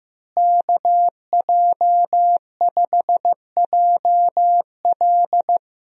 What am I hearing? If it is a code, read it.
Morse code, heard as a keyed tone, about 15 wpm, 700 Hz. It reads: KJ5JL
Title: USA Call Signs - 5 Characters - Repeated 3x - 15wpm